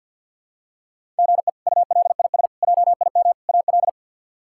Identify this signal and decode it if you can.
Morse code, heard as a keyed tone, about 50 wpm, 700 Hz. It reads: GE ULIS JIM AL
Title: K1USN SST QSO Practice - 50wpm